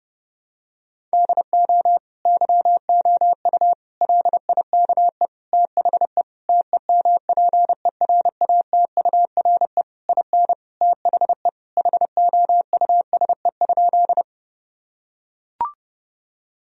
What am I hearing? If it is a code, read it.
Morse code, heard as a keyed tone, about 32 wpm, 700 Hz. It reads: DO YOU LIKE THE TEMPERATURE IN THE HOUSE?
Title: Sentences from Top 1000 Words - Review Part 4 - 30wpm